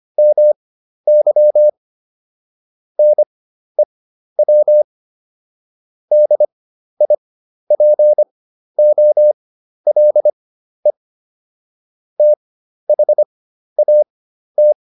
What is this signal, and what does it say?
Morse code, heard as a keyed tone, about 25 wpm, spaced as for 12 wpm, 600 Hz. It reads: MY NEW DIPOLE THAT